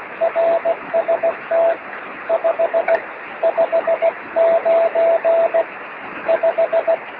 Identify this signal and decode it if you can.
Morse code, heard as a keyed tone, about 16 wpm, 660 Hz, with radio noise. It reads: RST 559 5
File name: sm7lcb / 27-es0u 23cm